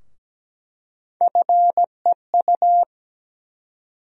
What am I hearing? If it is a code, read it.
Morse code, heard as a keyed tone, about 17 wpm, 700 Hz. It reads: FEU
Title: USA Call Sign Postfixes - 3 Letters 17wpm